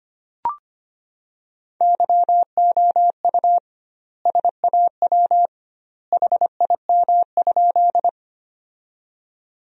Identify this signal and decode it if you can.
Morse code, heard as a keyed tone, about 25 wpm, 700 Hz. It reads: YOU SAW HIM?
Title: 2x Word-Spacing - Sentences from Top 300 Words - 25wpm